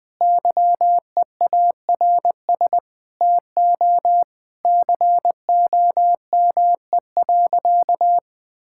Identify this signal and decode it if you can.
Morse code, heard as a keyed tone, about 20 wpm, 700 Hz. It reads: YEARS TO COME.
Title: Sentences with 7-Word Phrases - 20wpm